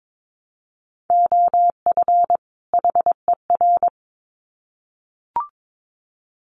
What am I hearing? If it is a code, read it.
Morse code, heard as a keyed tone, about 22 wpm, 700 Hz. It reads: OF HER
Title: Sets of 2 Words 22wpm